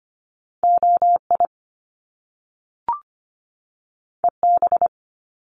Morse code, heard as a keyed tone, about 25 wpm, 700 Hz. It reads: OI EB
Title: Top Two Letter Combinations in Words 25wpm